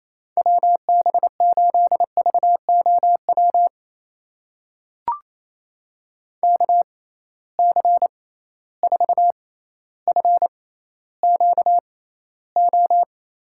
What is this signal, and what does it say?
Morse code, heard as a keyed tone, about 28 wpm, spaced as for 14 wpm, 700 Hz. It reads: WB8VOW KC4FQO
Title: USA Call Signs - 6 Characters - 6x Character Spacing - 28wpm